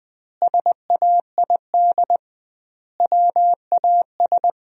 Morse code, heard as a keyed tone, about 20 wpm, 700 Hz. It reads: SAID WAS
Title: 2x Word-Spacing - Sentences from Top 400 Words - 20wpm